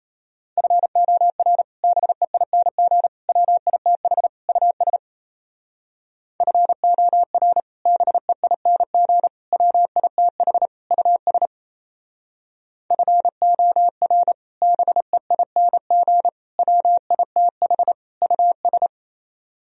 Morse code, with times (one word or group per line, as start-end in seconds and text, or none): 0.6-1.6 FOR
1.8-3.1 BEING
3.3-4.3 WITH
4.5-5.0 US
6.4-7.6 FOR
7.9-9.3 BEING
9.5-10.7 WITH
10.9-11.5 US
12.9-14.3 FOR
14.6-16.3 BEING
16.6-17.9 WITH
18.2-18.9 US